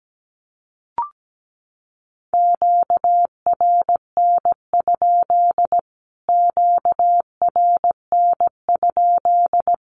Morse code, heard as a keyed tone, about 17 wpm, 700 Hz. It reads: QRN? QRN?
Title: Q-Codes Repeated 3x 17wpm